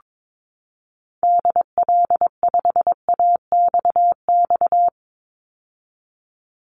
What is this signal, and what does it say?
Morse code, heard as a keyed tone, about 22 wpm, 700 Hz. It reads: DL5AXX
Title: Call Signs - 22wpm